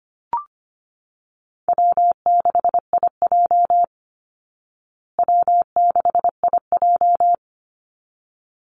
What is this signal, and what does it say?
Morse code, heard as a keyed tone, about 25 wpm, 700 Hz. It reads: W6IJ W6IJ
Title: Easy Call Signs Repeated 2x - 25wpm